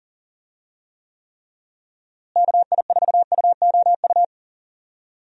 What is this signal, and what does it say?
Morse code, heard as a keyed tone, about 40 wpm, 700 Hz. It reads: KI4UOU
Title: USA Call Signs - 6 Characters - Repeated 3x - 40wpm